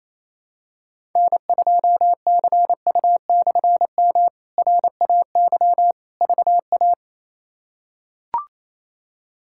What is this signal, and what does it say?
Morse code, heard as a keyed tone, about 28 wpm, 700 Hz. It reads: N2CU/M RAY VA
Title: K1USN SST QSO Practice - 28wpm